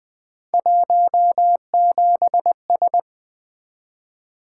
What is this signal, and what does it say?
Morse code, heard as a keyed tone, about 24 wpm, 700 Hz. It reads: 17S
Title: Speed-Racing - Three Letter-Number Combinations - 20wpm